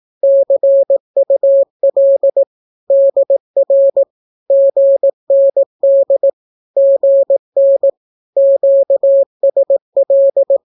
Morse code, heard as a keyed tone, about 18 wpm, 550 Hz. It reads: CUL DR GND GN QSL
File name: naukapodstaw / skroty / skroty